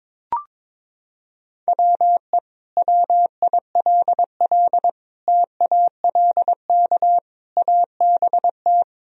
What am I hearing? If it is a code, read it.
Morse code, heard as a keyed tone, about 22 wpm, 700 Hz. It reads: WE WILL TALK ABT
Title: Sentences from Top 400 Words and Abbreviations 22wpm